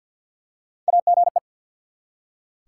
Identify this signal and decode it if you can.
Morse code, heard as a keyed tone, about 48 wpm, 700 Hz. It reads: AGE
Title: Three Letter Words - 50wpm